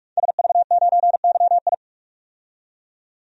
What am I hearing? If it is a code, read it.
Morse code, heard as a keyed tone, about 46 wpm, 700 Hz. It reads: SV9YI
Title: DXCC-100 Call Signs - 5 Characters - 45wpm